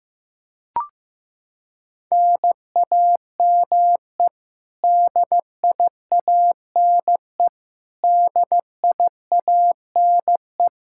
Morse code, heard as a keyed tone, about 16 wpm, 700 Hz. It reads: NAME DIANE DIANE
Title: Practice QSO - Exchange 1 - Beginning - 15wpm